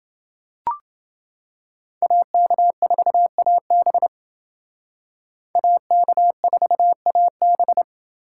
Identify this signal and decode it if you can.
Morse code, heard as a keyed tone, about 28 wpm, 700 Hz. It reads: AK4AB AK4AB